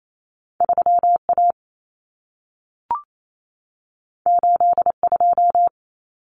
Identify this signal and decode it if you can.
Morse code, heard as a keyed tone, about 28 wpm, 700 Hz. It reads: 3A 82
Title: Common Cut Numbers - 2-Digit 28wpm